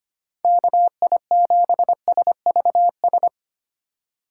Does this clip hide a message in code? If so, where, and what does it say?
Yes: Morse code from 0.4 to 3.3 s, heard as a keyed tone, about 32 wpm, 700 Hz. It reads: KI7SVS